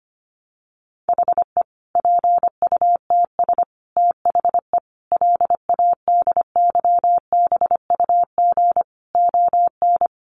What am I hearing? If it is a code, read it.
Morse code, heard as a keyed tone, about 25 wpm, 700 Hz. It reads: HE PUTS THE LADYBUG ON
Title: Additional Easy Phrases 25wpm